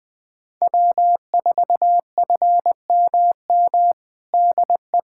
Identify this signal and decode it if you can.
Morse code, heard as a keyed tone, about 20 wpm, 700 Hz. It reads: W4FMM DE
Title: USA Callsign DE Callsign - 5 Characters - 20wpm